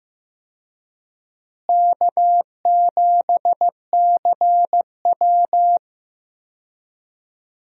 Morse code, heard as a keyed tone, about 16 wpm, 700 Hz. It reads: K7CW
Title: USA Call Signs - 4 Characters - Repeated 2x - 15wpm